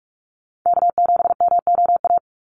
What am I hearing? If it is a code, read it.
Morse code, heard as a keyed tone, about 44 wpm, 700 Hz. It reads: K7MOA
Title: Call Signs Repeated 2x - 45wpm